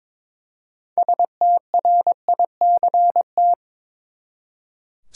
Morse code, heard as a keyed tone, about 21 wpm, 700 Hz. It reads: STRICT
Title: Six Letter Words - 22wpm